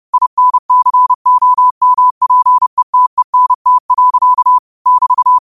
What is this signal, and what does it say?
Morse code, heard as a keyed tone, about 30 wpm, 1000 Hz. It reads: INCOMPETENT. <BT>
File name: qotdfiles / 30wpm